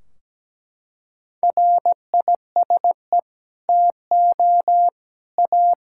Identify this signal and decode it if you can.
Morse code, heard as a keyed tone, about 17 wpm, 700 Hz. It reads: RISE TO A
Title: Sentences from Top 1600 Words 17wpm